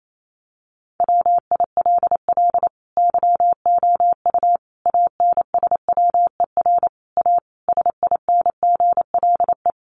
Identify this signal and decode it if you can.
Morse code, heard as a keyed tone, about 28 wpm, 700 Hz. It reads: WILL YOU ANSWER A SINGLE
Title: Sentences from Top 700 Words 28wpm